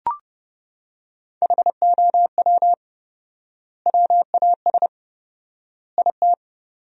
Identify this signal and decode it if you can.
Morse code, heard as a keyed tone, about 30 wpm, 700 Hz. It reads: HOW WAS IT
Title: 4x Word-Spacing - Sentences from Top 500 Words - 30wpm